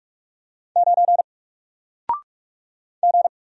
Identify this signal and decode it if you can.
Morse code, heard as a keyed tone, about 46 wpm, 700 Hz. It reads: 9 G
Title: Single Character (Letter:Number:Punctuation:Prosign) Alt 45wpm